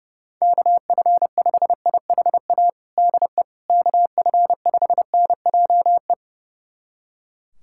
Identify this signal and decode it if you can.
Morse code, heard as a keyed tone, about 30 wpm, 700 Hz. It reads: KF5IHA DE KF5NJE
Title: USA Callsign DE Callsign 30wpm